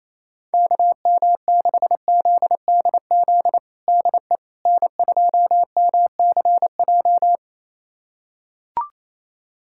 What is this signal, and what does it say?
Morse code, heard as a keyed tone, about 28 wpm, 700 Hz. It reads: KM6ZDZ DE N2MCJ
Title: USA Callsign DE Callsign 28wpm